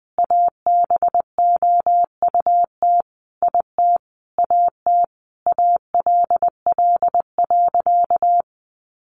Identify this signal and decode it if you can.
Morse code, heard as a keyed tone, about 20 wpm, 700 Hz. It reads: ABOUT IT AT ALL.
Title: Sentences from Top 800 Words 20wpm